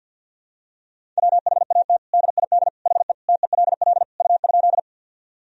Morse code, heard as a keyed tone, about 50 wpm, 700 Hz. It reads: WHAT DID HE TELL U?
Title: Sentences from Top 200 Words and Abbreviations 50wpm